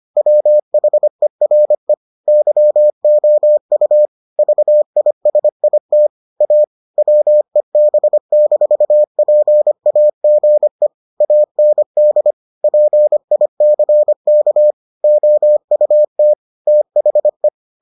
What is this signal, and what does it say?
Morse code, heard as a keyed tone, about 25 wpm, 600 Hz. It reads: WHERE YOU VISIT A WEB-PAGE AND PICK OUT THE